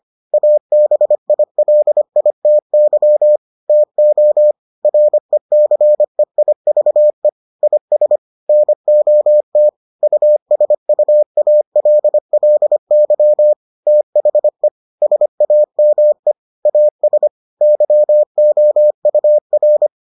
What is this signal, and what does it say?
Morse code, heard as a keyed tone, about 25 wpm, 600 Hz. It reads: ABILITY TO RECEIVE IS NOT USUALLY THE SAME AS YOUR